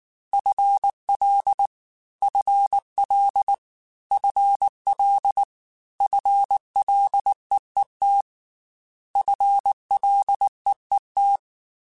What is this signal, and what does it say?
Morse code, heard as a keyed tone, about 19 wpm, 800 Hz. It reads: FL FL FL FLEET FLEET
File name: FL / FL mp3s / FL Prefixes 3x2 @18